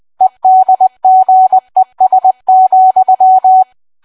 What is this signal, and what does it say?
Morse code, heard as a keyed tone, about 20 wpm, 750 Hz. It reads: EDGES,